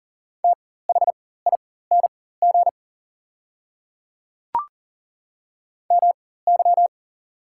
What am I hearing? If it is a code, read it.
Morse code, heard as a keyed tone, about 40 wpm, spaced as for 19 wpm, 690 Hz. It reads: THING MY
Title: 4x Character Spacing - Top 100 Words - 40wpm